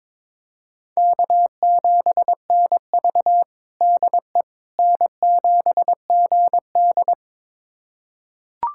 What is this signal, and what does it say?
Morse code, heard as a keyed tone, about 22 wpm, 700 Hz. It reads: K7NV DE N7GD
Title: USA Callsign DE Callsign - 4 Characters - 22wpm